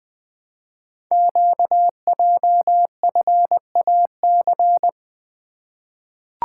Morse code, heard as a keyed tone, about 20 wpm, 700 Hz. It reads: QJFAC
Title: Code Groups - 5 Letters - 20wpm